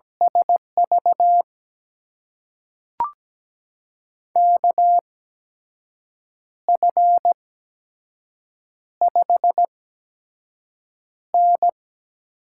Morse code, heard as a keyed tone, about 17 wpm, spaced as for 6 wpm, 700 Hz. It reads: SV KF5N
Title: USA Call Signs - 6 Characters - 8x Character Spacing - 17wpm